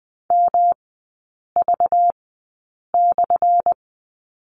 Morse code, heard as a keyed tone, about 20 wpm, spaced as for 8 wpm, 690 Hz. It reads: MV/
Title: 212 Lesson 34e - Number 8 - Three Characters 20wpm